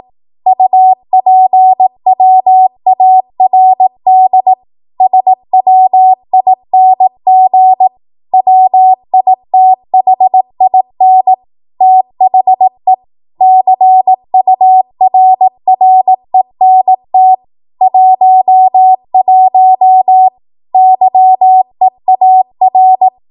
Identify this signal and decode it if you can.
Morse code, heard as a keyed tone, about 18 wpm, 750 Hz. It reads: UPWARD SWING WITHIN THE CURRENT 11 YEAR
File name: Morse / 18WPM